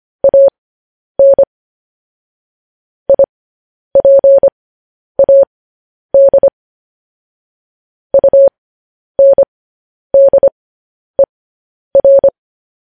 Morse code, heard as a keyed tone, about 25 wpm, spaced as for 10 wpm, 550 Hz. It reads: AN IPAD UNDER
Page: news-headlines-in-morse-code-at-10-wpm